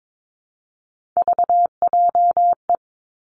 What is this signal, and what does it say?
Morse code, heard as a keyed tone, about 22 wpm, 700 Hz. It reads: VJE
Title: Three Challenging Letter-Number Combinations 22wpm